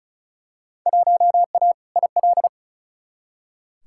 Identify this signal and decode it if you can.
Morse code, heard as a keyed tone, about 35 wpm, 700 Hz. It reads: 1A IL
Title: Field Day Exchanges Encore - 35wpm